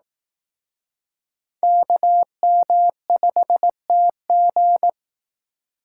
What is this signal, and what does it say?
Morse code, heard as a keyed tone, about 19 wpm, 700 Hz. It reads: KM5TG
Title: Speed Racing - USA 2x2 Call Signs 15wpm